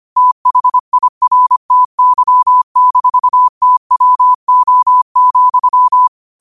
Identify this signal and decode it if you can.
Morse code, heard as a keyed tone, about 25 wpm, 1000 Hz. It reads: THIRTY-TWO,
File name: qotdfiles / 25wpm